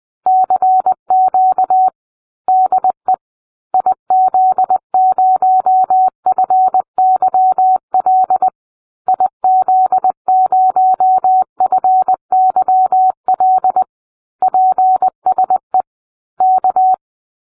Morse code, heard as a keyed tone, about 20 wpm, 750 Hz. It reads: CQ DE IZ0FYL IZ0FYL PSE K